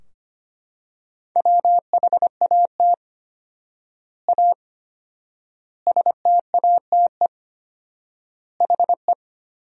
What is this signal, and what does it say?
Morse code, heard as a keyed tone, about 25 wpm, 700 Hz. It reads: WHAT A STATE HE
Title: 4x Word-Spacing - Sentences from Top 400 Words - 25wpm